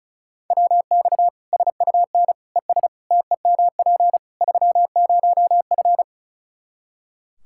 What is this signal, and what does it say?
Morse code, heard as a keyed tone, about 36 wpm, 700 Hz. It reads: WX SUN ES TEMP 30F
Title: Practice QSO - Exchange 2 - Info - 35wpm